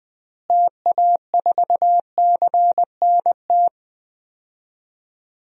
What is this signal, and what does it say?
Morse code, heard as a keyed tone, about 20 wpm, 700 Hz. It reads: TA4CNT
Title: DXCC-100 Call Signs - 6 Characters - Repeated 2x - 20wpm